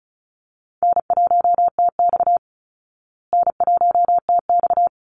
Morse code, heard as a keyed tone, about 34 wpm, 700 Hz. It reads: N1TX N1TX
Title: Easy Call Signs Repeated 2x - 35wpm